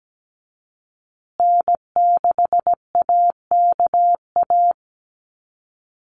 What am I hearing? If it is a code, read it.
Morse code, heard as a keyed tone, about 17 wpm, 700 Hz. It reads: N6AKA
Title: USA Call Signs Repeated 2x 17wpm